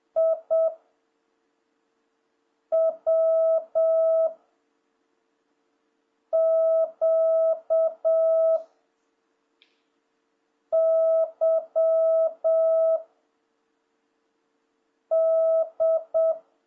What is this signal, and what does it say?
Morse code, heard as a keyed tone, about 7 wpm, 660 Hz. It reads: I W Q Y D
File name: jh8cbh / 2026cw